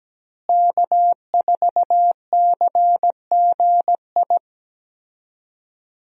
Morse code, heard as a keyed tone, about 17 wpm, 700 Hz. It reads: K4CGI
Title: USA Call Signs - 5 Characters - Repeated 3x - 17wpm